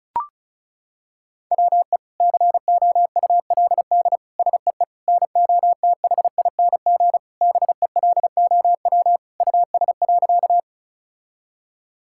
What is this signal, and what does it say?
Morse code, heard as a keyed tone, about 34 wpm, 700 Hz. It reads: WE COULD SEE NOTHING BELOW US.
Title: Sentences from Top 1000 Words - Review Part 5 - 35wpm